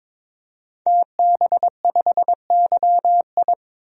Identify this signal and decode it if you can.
Morse code, heard as a keyed tone, about 22 wpm, 700 Hz. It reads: TB5YI
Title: DXCC-100 Call Signs - 5 Characters - 22wpm